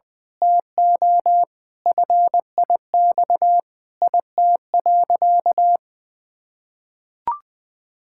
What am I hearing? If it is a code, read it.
Morse code, heard as a keyed tone, about 20 wpm, 700 Hz. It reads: TO FIX IT.
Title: Sentences with 7-Word Phrases - 20wpm